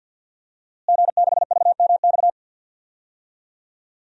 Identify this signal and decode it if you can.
Morse code, heard as a keyed tone, about 46 wpm, 700 Hz. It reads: G6VMX
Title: DXCC-100 Call Signs - 5 Characters - Repeated 2x - 50wpm